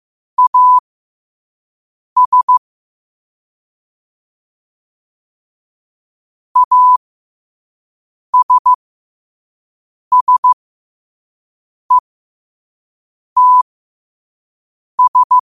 Morse code, heard as a keyed tone, about 15 wpm, spaced as for 5 wpm, 1000 Hz. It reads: AS ASSETS
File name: qotdfiles / 5wpm